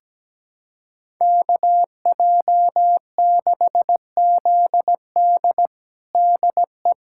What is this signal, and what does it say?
Morse code, heard as a keyed tone, about 17 wpm, 700 Hz. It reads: KJ6ZD DE
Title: USA Callsign DE Callsign - 5 Characters - 17wpm